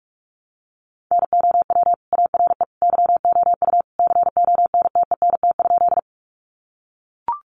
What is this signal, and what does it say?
Morse code, heard as a keyed tone, about 44 wpm, 700 Hz. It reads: NOW ARE YOU CONTENT?
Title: Sentences from Top 900 Words 45wpm